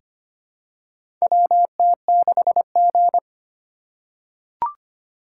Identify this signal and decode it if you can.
Morse code, heard as a keyed tone, about 25 wpm, 700 Hz. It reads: WT6G
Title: USA Call Signs - 2x Character Spacing - 25wpm